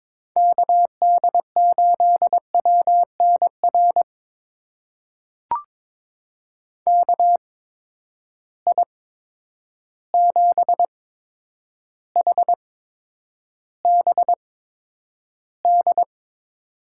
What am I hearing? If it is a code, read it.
Morse code, heard as a keyed tone, about 22 wpm, spaced as for 9 wpm, 700 Hz. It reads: KD8WNR KI7HBD
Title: USA Call Signs - 6 Characters - 8x Character Spacing - 22wpm